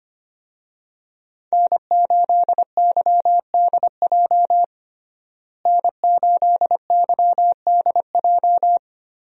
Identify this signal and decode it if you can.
Morse code, heard as a keyed tone, about 25 wpm, 700 Hz. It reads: N8YDJ N8YDJ
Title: USA Call Signs - 5 Characters - Repeated 3x - 25wpm